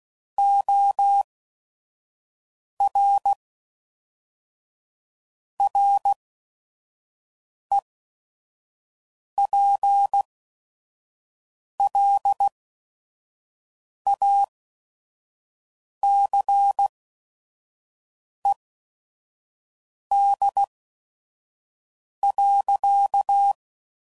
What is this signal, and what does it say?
Morse code, heard as a keyed tone, about 16 wpm, spaced as for 5 wpm, 800 Hz. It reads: OR REPLACED.
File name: Sayings / Chuckles / Squeaky